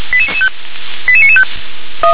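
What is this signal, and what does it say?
Five-tone selective call (ZVEI), heard as quick runs of five tones, tones of about 70 ms. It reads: 88995
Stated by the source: ZVEI five-tone call received on an SDR.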